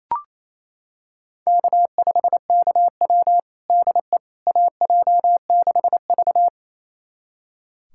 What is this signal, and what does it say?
Morse code, heard as a keyed tone, about 28 wpm, 700 Hz. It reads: K5KW DE AJ6V
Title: USA Callsign DE Callsign - 4 Characters - 28wpm